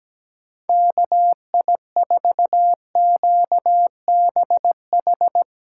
Morse code, heard as a keyed tone, about 17 wpm, 700 Hz. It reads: KI4QBH
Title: Speed Racing - USA 2x3 Call Signs 17wpm